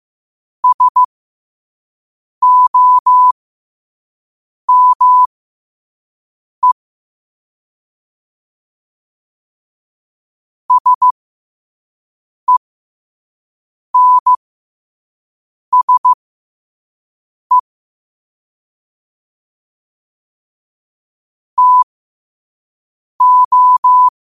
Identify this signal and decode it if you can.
Morse code, heard as a keyed tone, about 15 wpm, spaced as for 5 wpm, 1000 Hz. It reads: SOME SENSE TO
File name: qotdfiles / 5wpm